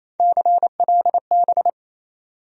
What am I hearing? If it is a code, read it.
Morse code, heard as a keyed tone, about 28 wpm, 700 Hz. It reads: CLB
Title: USA Call Sign Postfixes - 3 Letters 28wpm